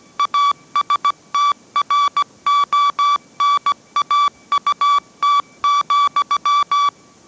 Morse code, heard as a keyed tone, about 17 wpm, 1150 Hz. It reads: ASTRONAUT,
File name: mrx / mrx080 18wpm